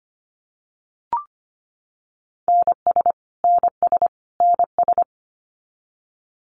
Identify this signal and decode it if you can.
Morse code, heard as a keyed tone, about 25 wpm, 700 Hz. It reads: NS NS NS
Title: Canadian Provinces and Territories - Repeated 3x - 25wpm